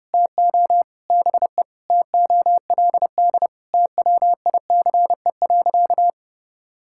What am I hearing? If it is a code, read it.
Morse code, heard as a keyed tone, about 30 wpm, 700 Hz. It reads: TO BE TOLD TWICE.
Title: Sentences from Top 1600 Words 30wpm